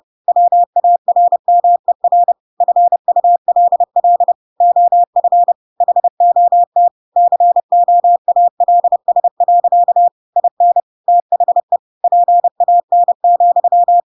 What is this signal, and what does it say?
Morse code, heard as a keyed tone, about 30 wpm, 700 Hz. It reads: WARMER FULL OF HOT COALS. IN THE PAN,
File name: Pinocchio / Pinocchio ch5 30 30 ik1hge